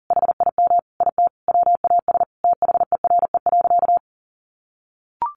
Morse code, heard as a keyed tone, about 40 wpm, 700 Hz. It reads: HIM IT WAS THERE.
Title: A Sound Of Thunder 40wpm